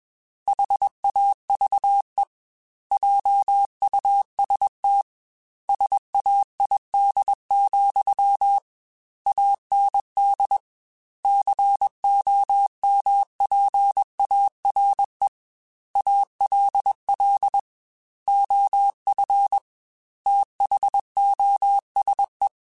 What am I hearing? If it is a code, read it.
Morse code, heard as a keyed tone, about 21 wpm, 800 Hz. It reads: HAVE JUST SAID, AND COMPARE ALL OF THOSE
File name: Technical Subjects / Technical Subjects-mp3s / OR Gate @20